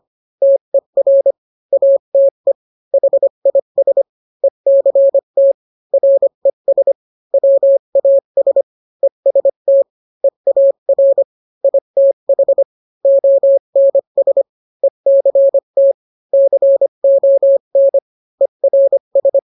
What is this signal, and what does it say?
Morse code, heard as a keyed tone, about 25 wpm, 550 Hz. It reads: TER ATE HIS ECT RES WAS EST EAR ITH ONS ECT CON ERS